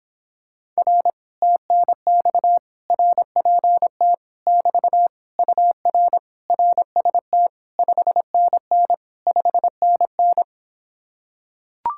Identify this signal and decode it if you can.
Morse code, heard as a keyed tone, about 26 wpm, 700 Hz. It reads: R TNX RPT <BT> UR RST 5NN 5NN